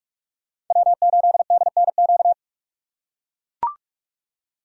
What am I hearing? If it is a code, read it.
Morse code, heard as a keyed tone, about 44 wpm, 690 Hz. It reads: W8DNQ